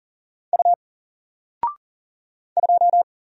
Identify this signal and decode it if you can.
Morse code, heard as a keyed tone, about 40 wpm, 700 Hz. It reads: U 2
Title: Single Character (Letter:Number:Punctuation:Prosign) 40wpm